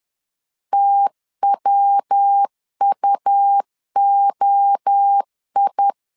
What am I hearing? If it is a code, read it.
Morse code, heard as a keyed tone, about 10 wpm, 800 Hz. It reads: TWUOI
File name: LEZIONI / 4 Q 10wpm 2210